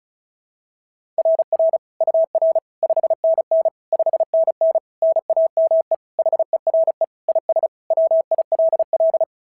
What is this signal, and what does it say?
Morse code, heard as a keyed tone, about 36 wpm, 690 Hz. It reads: RR UR 5NN 5NN NAME HERE IS WILL
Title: SOTA QSO Practice 35wpm